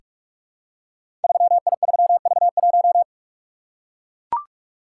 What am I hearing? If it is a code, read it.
Morse code, heard as a keyed tone, about 44 wpm, 700 Hz. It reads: 3I3V1
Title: Code Groups - 5 Letter:Numbers - 45wpm